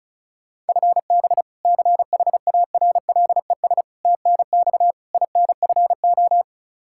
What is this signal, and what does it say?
Morse code, heard as a keyed tone, about 36 wpm, 700 Hz. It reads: FB CHARLES TNX INFO
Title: Practice QSO - Exchange 3 - More Info - 35wpm